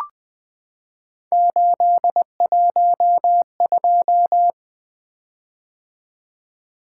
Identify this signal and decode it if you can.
Morse code, heard as a keyed tone, about 20 wpm, 700 Hz. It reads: 812
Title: Three Letter-Number Combinations - 20wpm